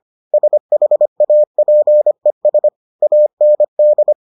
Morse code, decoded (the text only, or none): SHAPES AND